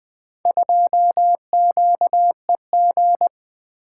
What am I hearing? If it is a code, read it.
Morse code, heard as a keyed tone, about 20 wpm, 700 Hz. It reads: 2QEG
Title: Four Letter-Number Combinations - 20wpm